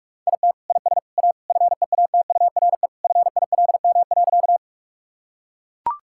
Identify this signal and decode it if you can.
Morse code, heard as a keyed tone, about 41 wpm, 700 Hz. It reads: IT IS A FEATURE FILM.
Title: Sentences from Top 1000 Words - Review Part 5 - 45wpm